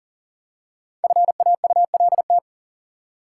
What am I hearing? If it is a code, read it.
Morse code, heard as a keyed tone, about 37 wpm, 700 Hz. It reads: FAULT
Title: Five Letter Words - 40wpm